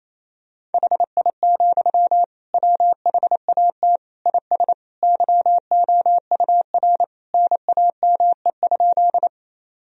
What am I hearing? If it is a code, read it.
Morse code, heard as a keyed tone, about 28 wpm, 700 Hz. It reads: HI, WHAT IS YOUR NAME?
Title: Sentences from Top 1000 Words - Review Part 3 - 28wpm